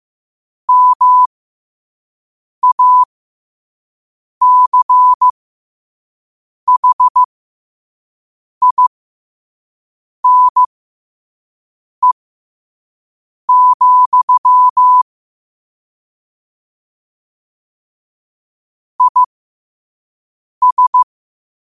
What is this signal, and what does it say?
Morse code, heard as a keyed tone, about 15 wpm, spaced as for 5 wpm, 1000 Hz. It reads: MACHINE, IS
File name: qotdfiles / 5wpm